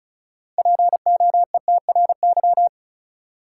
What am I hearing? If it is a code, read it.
Morse code, heard as a keyed tone, about 36 wpm, 700 Hz. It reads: POETRY
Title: Six Letter Words - 35wpm